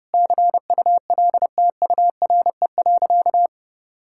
Morse code, heard as a keyed tone, about 30 wpm, 700 Hz. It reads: CULTURE.
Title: Sentences from Top 1000 Words - Review Part 6 - 30wpm